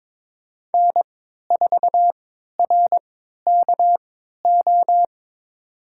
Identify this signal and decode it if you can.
Morse code, heard as a keyed tone, about 22 wpm, spaced as for 13 wpm, 700 Hz. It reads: N4RKO
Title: USA Call Signs - 5 Characters - 3x Character Spacing - 22wpm